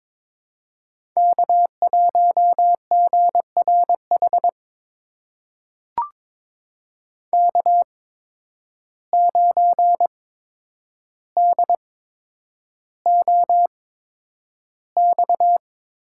Morse code, heard as a keyed tone, about 21 wpm, spaced as for 9 wpm, 700 Hz. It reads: K1GRH K9DOX
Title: USA Call Signs - 5 Characters - 8x Character Spacing - 22wpm